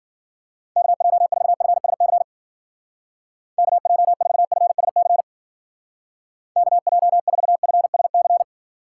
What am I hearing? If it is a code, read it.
Morse code, heard as a keyed tone, about 50 wpm, 700 Hz. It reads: KJ4FSC KJ4FSC KJ4FSC
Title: Speed Racing - USA 2x3 Call Signs 40wpm